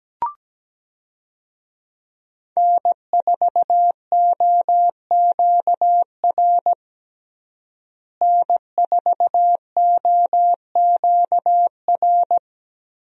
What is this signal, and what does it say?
Morse code, heard as a keyed tone, about 17 wpm, 700 Hz. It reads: N4OQR N4OQR
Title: USA Call Signs - 5 Characters - Repeated 2x - 17wpm